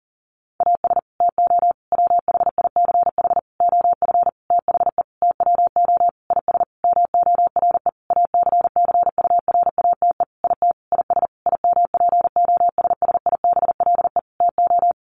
Morse code, heard as a keyed tone, about 40 wpm, 700 Hz. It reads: AS TO WHICH OF THE TWO IS MORE ACCURATE IT IS IMPOSSIBLE TO
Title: Sentences from Top 1200 Words 40wpm